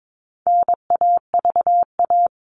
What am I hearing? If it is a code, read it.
Morse code, heard as a keyed tone, about 22 wpm, 700 Hz. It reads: NAVA
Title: All Cut Numbers - 4-Digit 22wpm